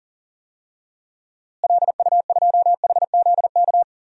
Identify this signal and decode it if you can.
Morse code, heard as a keyed tone, about 40 wpm, 700 Hz. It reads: LU2HZK
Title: DXCC-100 Call Signs - 6 Characters - Repeated 3x - 40wpm